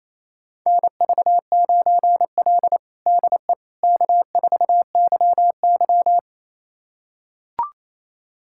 Morse code, heard as a keyed tone, about 28 wpm, 700 Hz. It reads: NV9L DE K4YY
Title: USA Callsign DE Callsign - 4 Characters - 28wpm